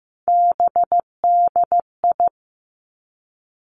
Morse code, heard as a keyed tone, about 15 wpm, 700 Hz. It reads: BDI
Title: Three Challenging Letter-Number Combinations 15wpm